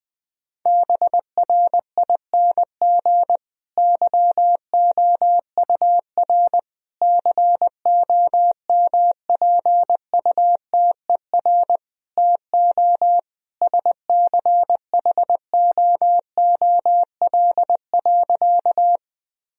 Morse code, heard as a keyed tone, about 20 wpm, 700 Hz. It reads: BRING YOUR COMPUTER TO SCHOOL.
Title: Sentences from Top 1000 Words - Review Part 3 - 20wpm